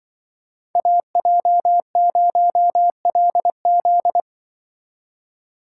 Morse code, heard as a keyed tone, about 24 wpm, 700 Hz. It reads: AJ0LZ
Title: Speed Racing - USA 2x2 Call Signs 20wpm